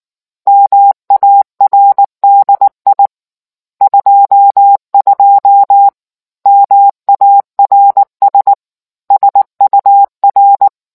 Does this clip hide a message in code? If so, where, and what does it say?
Yes: Morse code from 0.5 to 10.7 s, heard as a keyed tone, about 19 wpm, 800 Hz. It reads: MARDI 22 MARS SUR